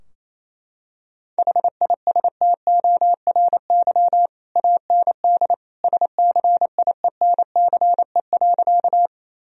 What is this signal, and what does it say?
Morse code, heard as a keyed tone, about 27 wpm, 700 Hz. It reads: HISTORY AND SCIENCE.
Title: Sentences from Top 1000 Words - Review Part 2 - 28wpm